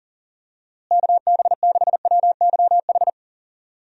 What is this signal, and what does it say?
Morse code, heard as a keyed tone, about 40 wpm, 700 Hz. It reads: KB6WYH
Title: USA Call Signs - 6 Characters - Repeated 3x - 40wpm